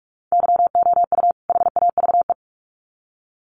Morse code, heard as a keyed tone, about 44 wpm, 700 Hz. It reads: YOU HAVE
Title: Sets of 2 Words 45wpm